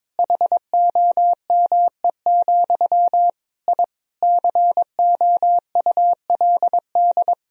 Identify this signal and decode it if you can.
Morse code, heard as a keyed tone, about 22 wpm, 700 Hz. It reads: HOME, I COULD
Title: Sentences from Top 1000 Words - Review Part 3 - 22wpm